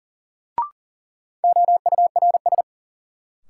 Morse code, heard as a keyed tone, about 40 wpm, 700 Hz. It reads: OURS
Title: Sentences from Top 1300 Words - New Word Drill Alt - 40wpm